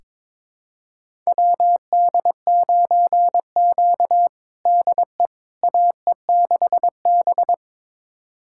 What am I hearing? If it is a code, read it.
Morse code, heard as a keyed tone, about 22 wpm, 700 Hz. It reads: WD9Q DE AE6B
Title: USA Callsign DE Callsign - 4 Characters - 22wpm